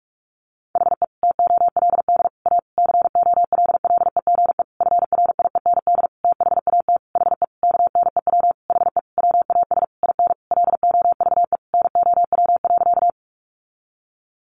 Morse code, heard as a keyed tone, about 44 wpm, 700 Hz. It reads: HE TOLD A COLLEGE FRIEND THAT HE KNEW HE WAS IN LOVE NOW.
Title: Sentences from Top 1300 Words 45wpm